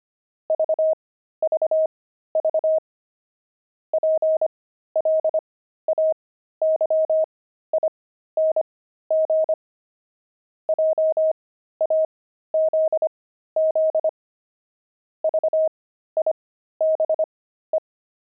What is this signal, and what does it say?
Morse code, heard as a keyed tone, about 25 wpm, spaced as for 13 wpm, 630 Hz. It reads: VVV PLAYING JAZZ VIBE
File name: pangram-fs-13-cs-25-jazz